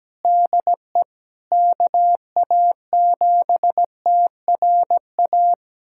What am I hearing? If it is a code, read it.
Morse code, heard as a keyed tone, about 17 wpm, 700 Hz. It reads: DE KA7TRA
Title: USA Callsign DE Callsign 17wpm